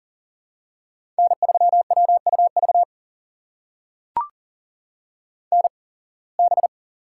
Morse code, heard as a keyed tone, about 40 wpm, spaced as for 26 wpm, 700 Hz. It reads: N3WUV NB
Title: USA Call Signs - 8x Character Spacing - 40wpm